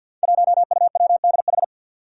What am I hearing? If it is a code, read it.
Morse code, heard as a keyed tone, about 53 wpm, 700 Hz. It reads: 1UWDH
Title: Five Challenging Letter-Number Combinations 50wpm